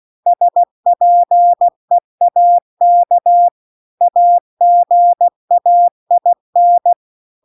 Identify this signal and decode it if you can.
Morse code, heard as a keyed tone, about 16 wpm, 700 Hz. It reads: SPEAK AGAIN